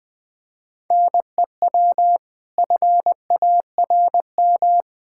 Morse code, heard as a keyed tone, about 20 wpm, 700 Hz. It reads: NEW FARM
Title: Pangram Challenge v2 - 20wpm